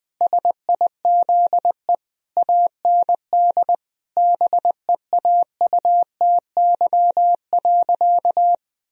Morse code, heard as a keyed tone, about 21 wpm, 700 Hz. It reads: SIZE AND BEAUTY.
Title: Sentences from Top 1000 Words - Review Part 2 - 20wpm